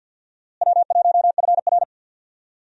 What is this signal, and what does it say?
Morse code, heard as a keyed tone, about 47 wpm, 700 Hz. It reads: W1FR
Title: USA Call Signs - 4 Characters - Repeated 3x - 50wpm